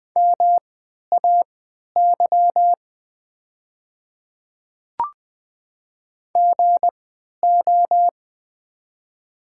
Morse code, heard as a keyed tone, about 20 wpm, spaced as for 11 wpm, 700 Hz. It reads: MAY GO
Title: 3x Character Spacing - Top 100 Words - 20wpm